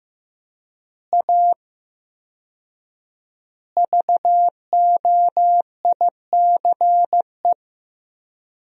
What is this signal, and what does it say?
Morse code, heard as a keyed tone, about 15 wpm, 700 Hz. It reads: A VOICE
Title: 4x Word-Spacing - Sentences from Top 500 Words - 15wpm